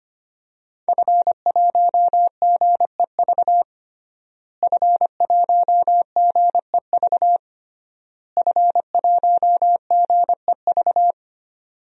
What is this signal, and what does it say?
Morse code, heard as a keyed tone, about 25 wpm, 700 Hz. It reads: F1GEV F1GEV F1GEV
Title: DXCC-100 Call Signs - 5 Characters - Repeated 3x - 25wpm